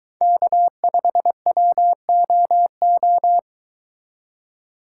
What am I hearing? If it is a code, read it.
Morse code, heard as a keyed tone, about 22 wpm, 700 Hz. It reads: K5WOO